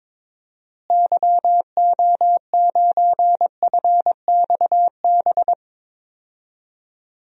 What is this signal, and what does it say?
Morse code, heard as a keyed tone, about 22 wpm, 700 Hz. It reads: YO9FXB
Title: DXCC-100 Call Signs - 6 Characters - Repeated 3x - 22wpm